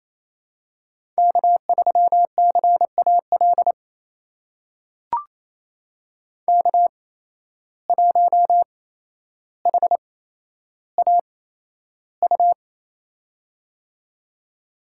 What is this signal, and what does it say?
Morse code, heard as a keyed tone, about 28 wpm, spaced as for 12 wpm, 700 Hz. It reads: K3CAL K1HAU
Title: USA Call Signs - 5 Characters - 8x Character Spacing - 28wpm